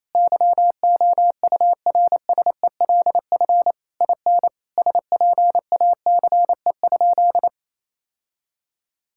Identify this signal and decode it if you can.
Morse code, heard as a keyed tone, about 27 wpm, 700 Hz. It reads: YOURSELF IN SPACE?
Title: Sentences from Top 1000 Words - Review Part 5 - 28wpm